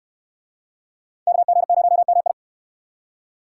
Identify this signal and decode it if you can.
Morse code, heard as a keyed tone, about 64 wpm, 700 Hz. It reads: KG0MI